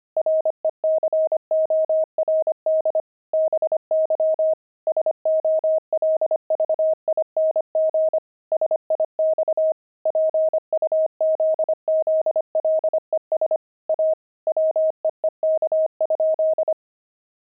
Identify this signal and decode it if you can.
Morse code, heard as a keyed tone, about 25 wpm, 630 Hz. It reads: RECORD BY SOLVING SIX PUZZLES A WEEK?
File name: pangram-fs-25-cs-25-puzzles